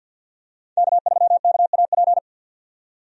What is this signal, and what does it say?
Morse code, heard as a keyed tone, about 53 wpm, 700 Hz. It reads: K3KAP
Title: USA Call Signs - 5 Characters - 50wpm